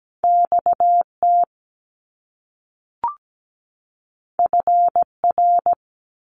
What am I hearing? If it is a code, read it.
Morse code, heard as a keyed tone, about 17 wpm, 700 Hz. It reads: XT FR
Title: Top Two Letter Combinations in Words 17wpm